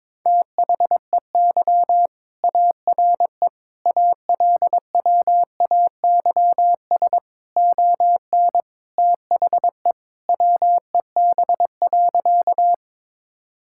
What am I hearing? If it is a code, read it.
Morse code, heard as a keyed tone, about 22 wpm, 700 Hz. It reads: THEY ARE ALWAYS ON THE WEB.
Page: morse-code-ninja-practice-22wpm